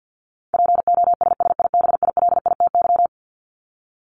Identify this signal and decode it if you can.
Morse code, heard as a keyed tone, about 50 wpm, 700 Hz. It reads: POSSIBILITY
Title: Top 1751-2000 Words 50wpm